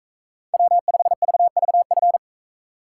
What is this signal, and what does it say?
Morse code, heard as a keyed tone, about 41 wpm, 700 Hz. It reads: W5VVF